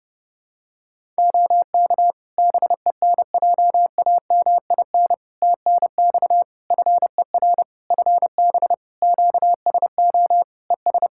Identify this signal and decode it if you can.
Morse code, heard as a keyed tone, about 30 wpm, 700 Hz. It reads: OK BENJAMIN TNX FER FB QSO ES
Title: Practice QSO - Exchange 4 - Ending - 30wpm